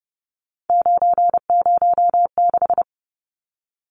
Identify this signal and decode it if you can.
Morse code, heard as a keyed tone, about 30 wpm, 700 Hz. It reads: 906
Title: Numbers - 3-Digit 30wpm